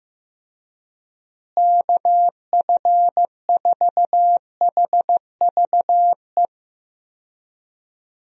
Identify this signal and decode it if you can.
Morse code, heard as a keyed tone, about 16 wpm, 700 Hz. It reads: KF4HVE
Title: USA Call Signs - 6 Characters - Repeated 2x - 15wpm